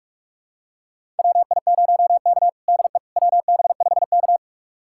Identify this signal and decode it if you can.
Morse code, heard as a keyed tone, about 42 wpm, 700 Hz. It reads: WI0K DE WB5K
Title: USA Callsign DE Callsign - 4 Characters - 45wpm